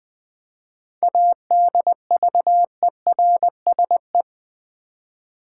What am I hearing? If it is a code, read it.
Morse code, heard as a keyed tone, about 20 wpm, 700 Hz. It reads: ADVERSE
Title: 7 Letter Words - First 3 Letters Given - 20wpm